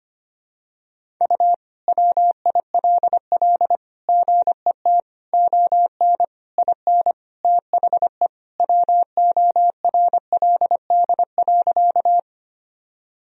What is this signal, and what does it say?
Morse code, heard as a keyed tone, about 25 wpm, 700 Hz. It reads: U WILL GET ON IN THE WORLD.
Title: Sentences from Top 200 Words and Abbreviations 25wpm